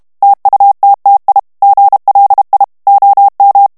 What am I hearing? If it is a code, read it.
Morse code, heard as a keyed tone, about 32 wpm, 800 Hz. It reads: TUTTI GLI OM
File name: ITALIEN texte clair 1 vitesse 30 mots minute